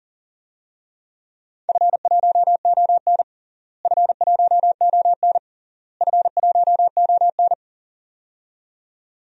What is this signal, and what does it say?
Morse code, heard as a keyed tone, about 40 wpm, 700 Hz. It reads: F1ON F1ON F1ON
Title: DXCC-100 Call Signs - 4 Characters - Repeated 3x - 40wpm